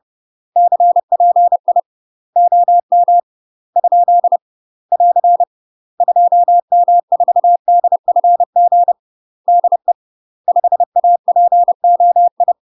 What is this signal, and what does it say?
Morse code, heard as a keyed tone, about 30 wpm, 700 Hz. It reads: CPI OM ? <AR> 2M4DFG DE 5APOI
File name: UK MASTER QSOs TRAINING / 30WPM / UK MASTER QSOs 0007